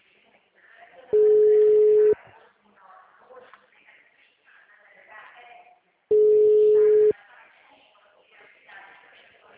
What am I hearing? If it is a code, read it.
Ringback tone (Europe, 425 Hz), with speech.